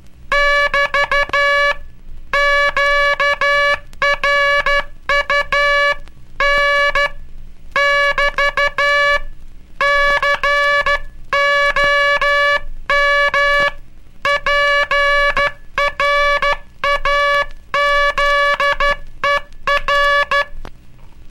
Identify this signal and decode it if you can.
Morse code, heard as a keyed tone, about 12 wpm, 1700 Hz, with noise. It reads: <BT> QRU N <BT> COM PRAZER
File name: cursos / Curso CW mp3 / CW Aula - 18